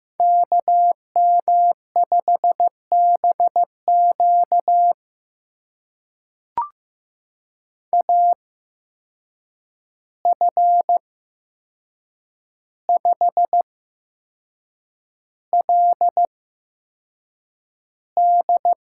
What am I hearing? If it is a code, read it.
Morse code, heard as a keyed tone, about 15 wpm, spaced as for 6 wpm, 700 Hz. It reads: KM5BQ AF5LD
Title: USA Call Signs - 5 Characters - 8x Character Spacing - 15wpm